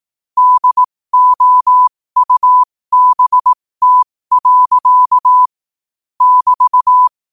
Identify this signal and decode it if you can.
Morse code, heard as a keyed tone, about 18 wpm, spaced as for 15 wpm, 1000 Hz. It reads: DOUBT. <BT>
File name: qotdfiles / 15wpm